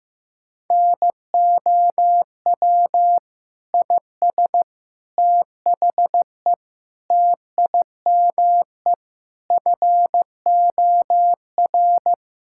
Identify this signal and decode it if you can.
Morse code, heard as a keyed tone, about 15 wpm, 700 Hz. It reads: NOW IS THE TIME FOR
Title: Pangram Challenge v2 - 15wpm